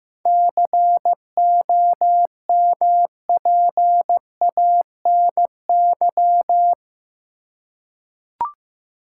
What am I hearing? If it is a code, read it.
Morse code, heard as a keyed tone, about 16 wpm, 700 Hz. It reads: COMPANY
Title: Seven Letter Words - 15wpm